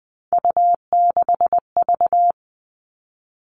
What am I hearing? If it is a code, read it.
Morse code, heard as a keyed tone, about 20 wpm, 700 Hz. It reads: U6V
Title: All Cut Numbers - 3-Digit 20wpm